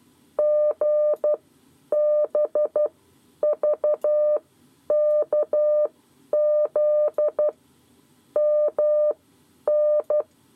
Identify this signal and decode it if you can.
Morse code, heard as a keyed tone, about 12 wpm, spaced as for 9 wpm, 600 Hz. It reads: GBVKZ MN
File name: 7 Lezione cw 2